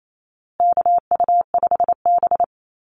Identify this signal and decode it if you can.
Morse code, heard as a keyed tone, about 28 wpm, 700 Hz. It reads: KU5B
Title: Call Signs Repeated 2x - 28wpm